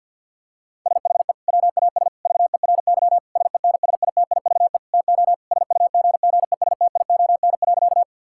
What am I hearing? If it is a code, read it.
Morse code, heard as a keyed tone, about 50 wpm, 700 Hz. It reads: SHE WAS VERY SENSITIVE TO SUGGESTION.
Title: Sentences from Top 1600 Words 50wpm